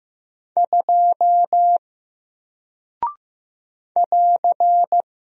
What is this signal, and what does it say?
Morse code, heard as a keyed tone, about 15 wpm, 700 Hz. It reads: 2 <AR>
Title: Single Character (Letter:Number:Punctuation:Prosign) 15wpm